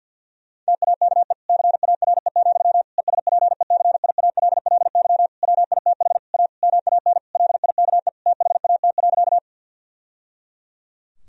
Morse code, heard as a keyed tone, about 50 wpm, 700 Hz. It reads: TAKE CARE, ESPECIALLY WITH A MAN LIKE THAT.
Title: Sentences from Top 1000 Words - Review Part 4 - 50wpm